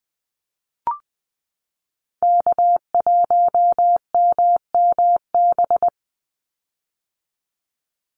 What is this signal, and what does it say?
Morse code, heard as a keyed tone, about 20 wpm, 700 Hz. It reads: K1MMB
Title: Call Signs - 20wpm